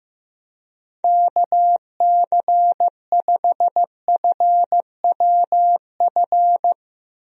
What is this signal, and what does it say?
Morse code, heard as a keyed tone, about 15 wpm, 700 Hz. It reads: KC5FWF
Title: USA Call Signs - 6 Characters - Repeated 2x - 15wpm